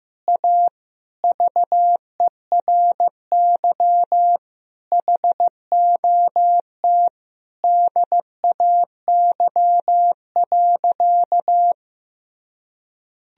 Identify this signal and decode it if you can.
Morse code, heard as a keyed tone, about 15 wpm, 700 Hz. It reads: A VERY HOT DAY.
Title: Sentences from Top 1000 Words - Review Part 3 - 15wpm